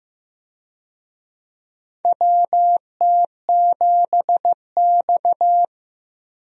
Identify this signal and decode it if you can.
Morse code, heard as a keyed tone, about 16 wpm, 700 Hz. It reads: WT7X
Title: USA Call Signs - 4 Characters - Repeated 3x - 15wpm